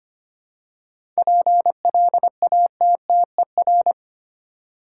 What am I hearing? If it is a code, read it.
Morse code, heard as a keyed tone, about 25 wpm, 690 Hz. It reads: PLATTER
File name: practice / 25wpm / Easy Words with Hints - 25wpm